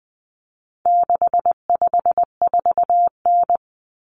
Morse code, heard as a keyed tone, about 20 wpm, 700 Hz. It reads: 654N
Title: Common Cut Numbers - 4-Digit 20wpm